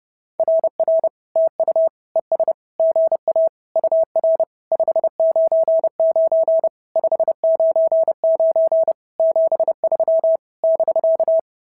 Morse code, heard as a keyed tone, about 30 wpm, 690 Hz. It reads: RR TU ES GA UR 599 599 73 <BK>
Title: SOTA QSO Practice 30wpm